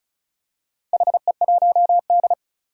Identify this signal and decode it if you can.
Morse code, heard as a keyed tone, about 36 wpm, 700 Hz. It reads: HE1D
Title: Four Challenging Letter-Number Combinations 35wpm